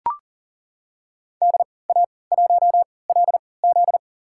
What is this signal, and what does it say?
Morse code, heard as a keyed tone, about 40 wpm, spaced as for 23 wpm, 700 Hz. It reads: DA1LZ
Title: USA Call Signs - 3x Character Spacing - 40wpm